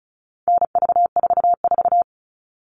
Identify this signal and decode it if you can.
Morse code, heard as a keyed tone, about 34 wpm, 700 Hz. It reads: NV44
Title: All Cut Numbers - 4-Digit 35wpm